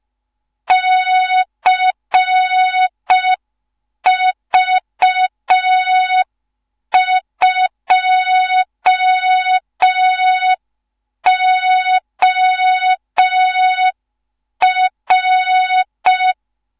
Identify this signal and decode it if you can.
Morse code, heard as a keyed tone, about 5 wpm, 3700 Hz. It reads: CV2OR
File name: Morze-HA2MN / hang / 15 GYAKORLAT